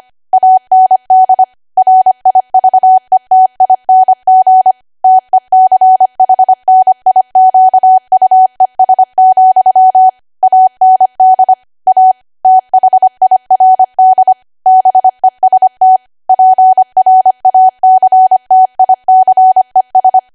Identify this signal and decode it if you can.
Morse code, heard as a keyed tone, about 25 wpm, 750 Hz. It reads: AND RIVETING TECHNIQUES, AND A THIRD BEST PRACTICES
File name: Archive / 25 WPM / 25WPM